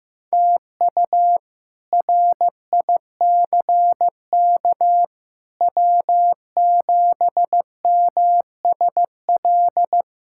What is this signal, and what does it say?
Morse code, heard as a keyed tone, about 15 wpm, 700 Hz. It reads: TU RICK W7MSL
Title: K1USN SST QSO Practice - 15wpm